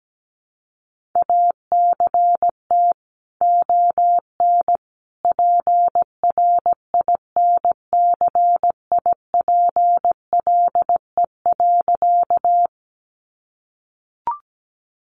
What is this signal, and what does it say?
Morse code, heard as a keyed tone, about 17 wpm, 700 Hz. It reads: ACT ON PRINCIPLE.
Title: Sentences from Top 1400 Words 17wpm